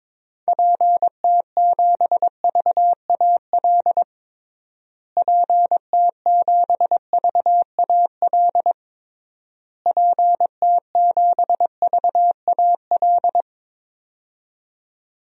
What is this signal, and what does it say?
Morse code, heard as a keyed tone, about 22 wpm, 700 Hz. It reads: PT7VAL PT7VAL PT7VAL
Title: DXCC-100 Call Signs - 6 Characters - Repeated 3x - 22wpm